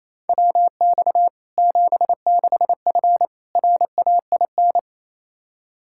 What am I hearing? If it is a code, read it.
Morse code, heard as a keyed tone, about 28 wpm, 700 Hz. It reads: WX 76F RAIN
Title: Practice QSO - Exchange 2 - Info - 28wpm